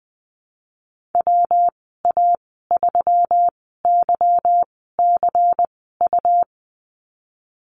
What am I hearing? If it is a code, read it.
Morse code, heard as a keyed tone, about 20 wpm, spaced as for 15 wpm, 700 Hz. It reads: WA3YCU
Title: 147 Lesson 24g - Letter y - Call Signs 20wpm